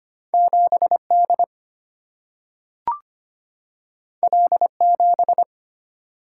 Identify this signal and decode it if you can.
Morse code, heard as a keyed tone, about 25 wpm, 700 Hz. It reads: 7D L7
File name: Two Letter-Number Combinations - 25wpm